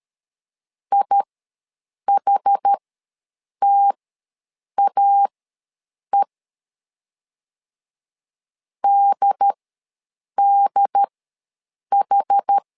Morse code, heard as a keyed tone, about 13 wpm, spaced as for 7 wpm, 800 Hz. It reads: IHTAE DDH
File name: LEZIONI / 2 UHDN / 2 UHD ws 12wpm 1722